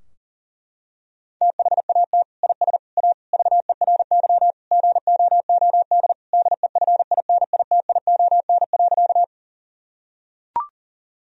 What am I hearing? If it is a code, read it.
Morse code, heard as a keyed tone, about 40 wpm, 700 Hz. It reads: THAT IS A VERY GOOD DEFINITION.
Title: Sentences from Top 1000 Words - Review Part 4 - 40wpm